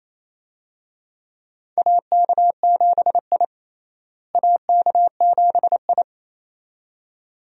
Morse code, heard as a keyed tone, about 28 wpm, 700 Hz. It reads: AK7I AK7I
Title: USA Call Signs - 4 Characters - Repeated 2x - 28wpm